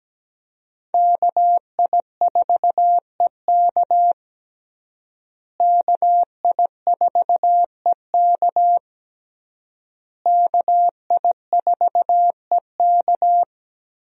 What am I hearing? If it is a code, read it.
Morse code, heard as a keyed tone, about 17 wpm, 700 Hz. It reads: KI4EK KI4EK KI4EK
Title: USA Call Signs - 5 Characters - Repeated 3x - 17wpm